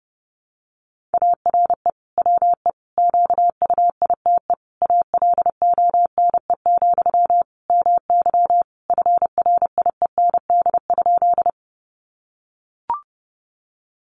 Morse code, heard as a keyed tone, about 30 wpm, 700 Hz. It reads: ARE WE QUITE ALONE, MY FRIEND?
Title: Sentences from Top 1000 Words 30wpm